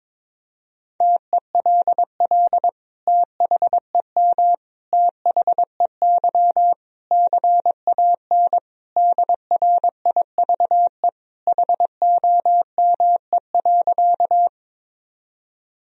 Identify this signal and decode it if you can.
Morse code, heard as a keyed tone, about 22 wpm, 700 Hz. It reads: TELL THEM THEY CAN DRIVE HOME.
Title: Sentences from Top 1000 Words - Review Part 2 - 22wpm